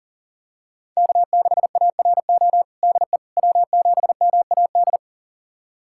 Morse code, heard as a keyed tone, about 37 wpm, 700 Hz. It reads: K6ARO DE W7MAD
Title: USA Callsign DE Callsign - 5 Characters - 40wpm